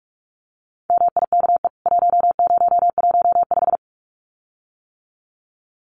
Morse code, heard as a keyed tone, about 44 wpm, 700 Hz. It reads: MIKE 1015
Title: CWT Exchange 45wpm